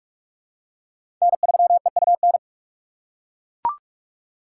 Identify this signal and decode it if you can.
Morse code, heard as a keyed tone, about 42 wpm, 700 Hz. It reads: N3EUN
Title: USA Call Signs - 4x Character Spacing - 45wpm